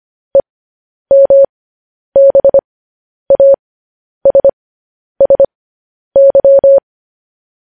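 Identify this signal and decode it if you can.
Morse code, heard as a keyed tone, about 25 wpm, spaced as for 10 wpm, 550 Hz. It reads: EMBASSY
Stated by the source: Morse news headlines sent at 10 wpm.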